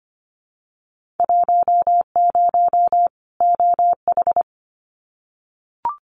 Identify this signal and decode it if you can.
Morse code, heard as a keyed tone, about 25 wpm, 700 Hz. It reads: 10 OH
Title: Winter Field Day Exchanges 25wpm